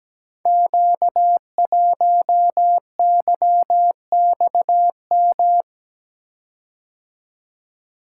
Morse code, heard as a keyed tone, about 17 wpm, 700 Hz. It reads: Q1YXM
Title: Code Groups - 5 Letter:Numbers - 17wpm